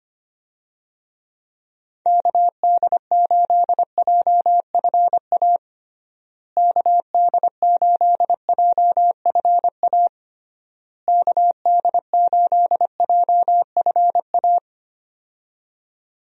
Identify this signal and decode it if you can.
Morse code, heard as a keyed tone, about 25 wpm, 700 Hz. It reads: KD8JFA KD8JFA KD8JFA
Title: USA Call Signs - 6 Characters - Repeated 3x - 25wpm